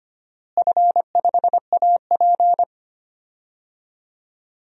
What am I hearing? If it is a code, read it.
Morse code, heard as a keyed tone, about 25 wpm, 700 Hz. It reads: F5AP
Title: DXCC-100 Call Signs - 4 Characters - Repeated 3x - 25wpm